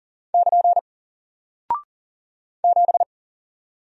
Morse code, heard as a keyed tone, about 40 wpm, 700 Hz. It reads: <KN> 7
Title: Single Character (Letter:Number:Punctuation:Prosign) Alt 40wpm